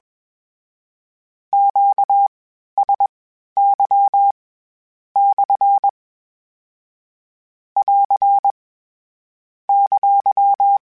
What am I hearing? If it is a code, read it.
Morse code, heard as a keyed tone, about 21 wpm, spaced as for 11 wpm, 800 Hz. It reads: QSY/ <AR> !
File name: Q Calls / Q Calls-mp3s / Q Calls SPSL1 @20